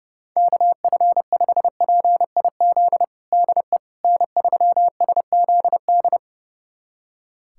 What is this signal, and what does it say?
Morse code, heard as a keyed tone, about 30 wpm, 700 Hz. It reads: KF5PIZ DE N3SZD
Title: USA Callsign DE Callsign 30wpm